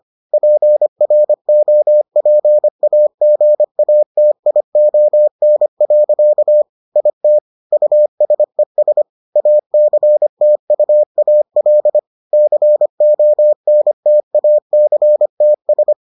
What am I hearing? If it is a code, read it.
Morse code, heard as a keyed tone, about 25 wpm, 600 Hz. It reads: PROPAGATION. IT USES ACTUAL CONTACTS